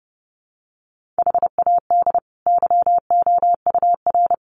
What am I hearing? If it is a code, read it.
Morse code, heard as a keyed tone, about 28 wpm, 700 Hz. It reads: HAD YOUR
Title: Sentences from Top 1500 Words 30wpm